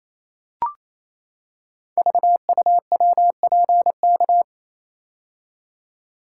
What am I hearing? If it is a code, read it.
Morse code, heard as a keyed tone, about 27 wpm, 700 Hz. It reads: VUWPK
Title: Code Groups - 5 Letters - 28wpm